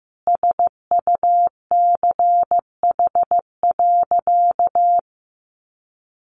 Morse code, heard as a keyed tone, about 15 wpm, 700 Hz. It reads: SUCH.
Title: Sentences from Top 1300 Words 15wpm